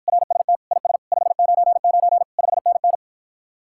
Morse code, heard as a keyed tone, about 53 wpm, 700 Hz. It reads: RST IS 599 5NN